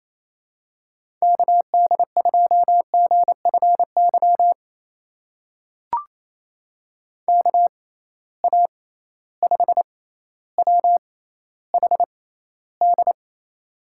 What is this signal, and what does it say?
Morse code, heard as a keyed tone, about 28 wpm, spaced as for 14 wpm, 700 Hz. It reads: KD2GFY KA5WHD
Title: USA Call Signs - 6 Characters - 6x Character Spacing - 28wpm